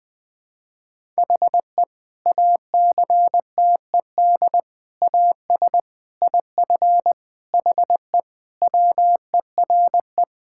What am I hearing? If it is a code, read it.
Morse code, heard as a keyed tone, about 20 wpm, 700 Hz. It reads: HE ACTED AS IF HE WERE
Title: Sentences with 4-Word Phrases - 20wpm